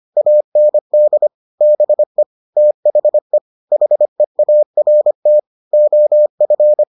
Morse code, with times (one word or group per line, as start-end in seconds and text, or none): 0.2-1.3 AND
1.6-2.2 BE
2.6-3.4 THE
3.7-5.4 HEART
5.7-6.8 OF